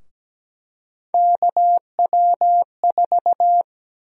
Morse code, heard as a keyed tone, about 17 wpm, 700 Hz. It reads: KW4
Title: USA Call Sign Prefixes - Repeated 2x - 17wpm